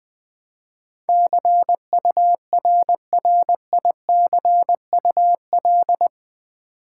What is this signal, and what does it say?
Morse code, heard as a keyed tone, about 20 wpm, 700 Hz. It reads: CURRICUL
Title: Build A Word - 4 Letters At A Time - 10-Letter Words - 20wpm